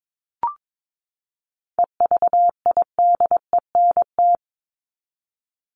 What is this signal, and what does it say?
Morse code, heard as a keyed tone, about 22 wpm, 700 Hz. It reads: EVIDENT
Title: LY Suffix - Top 100 Words with Base Words 22wpm